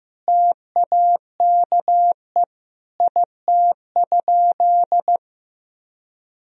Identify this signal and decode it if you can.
Morse code, heard as a keyed tone, about 15 wpm, 700 Hz. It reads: TAKE IT?
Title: Sentences from Top 300 Words and Abbreviations 15wpm